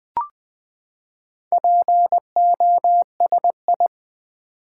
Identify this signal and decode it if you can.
Morse code, heard as a keyed tone, about 20 wpm, 700 Hz. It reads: POSI
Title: Build A Word - 4 Letters At A Time - 8-Letter Words - 20wpm